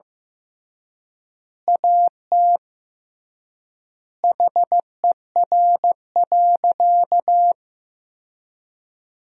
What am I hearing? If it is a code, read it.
Morse code, heard as a keyed tone, about 15 wpm, 700 Hz. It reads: AT HER.
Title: 3x Word-Spacing - Sentences from Top 200 Words - 15wpm